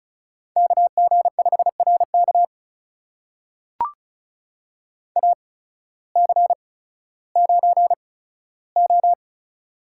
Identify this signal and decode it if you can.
Morse code, heard as a keyed tone, about 34 wpm, spaced as for 16 wpm, 700 Hz. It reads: KG5RK AC9O
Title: USA Call Signs - 8x Character Spacing - 35wpm